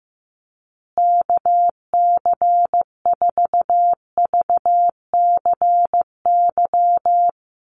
Morse code, heard as a keyed tone, about 15 wpm, 700 Hz. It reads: KC4VCY
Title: USA Call Signs Repeated 2x 15wpm